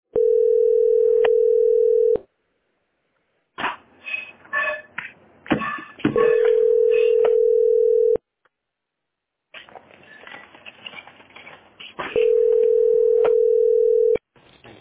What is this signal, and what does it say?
Ringback tone (North America), 440 + 480 Hz.